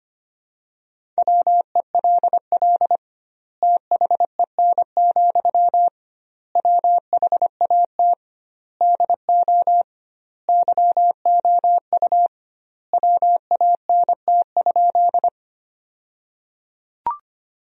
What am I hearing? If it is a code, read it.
Morse code, heard as a keyed tone, about 25 wpm, 700 Hz. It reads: WELL THEN, WHAT DO YOU WANT?
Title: 2x Word-Spacing - Sentences from Top 200 Words - 25wpm